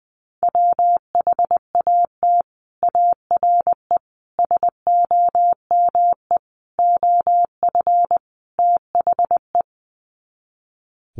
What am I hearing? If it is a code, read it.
Morse code, heard as a keyed tone, about 19 wpm, 700 Hz. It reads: WHAT ARE SOME OF THE
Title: Sets of 5 Words Encore 20wpm